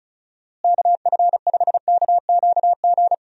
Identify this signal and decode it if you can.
Morse code, heard as a keyed tone, about 34 wpm, 700 Hz. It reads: KF5KQG
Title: USA Call Signs - 6 Characters - Repeated 2x - 35wpm